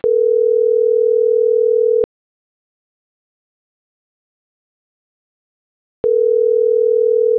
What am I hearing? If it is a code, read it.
Ringback tone (North America), 440 + 480 Hz.